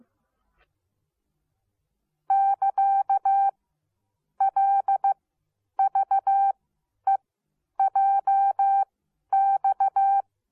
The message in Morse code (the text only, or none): <KA> LVEJX